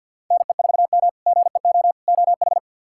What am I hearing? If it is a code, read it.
Morse code, heard as a keyed tone, about 45 wpm, 700 Hz. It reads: NE4M GEO OH
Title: K1USN SST QSO Practice - 50wpm